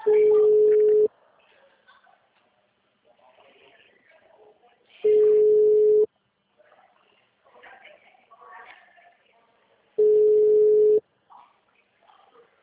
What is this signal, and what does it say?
Ringback tone (Europe, 425 Hz), with speech.